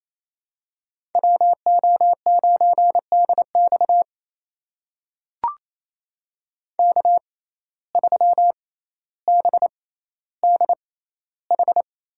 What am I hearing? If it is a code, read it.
Morse code, heard as a keyed tone, about 28 wpm, spaced as for 14 wpm, 700 Hz. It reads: WO9DX K3BDH
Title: USA Call Signs - 5 Characters - 6x Character Spacing - 28wpm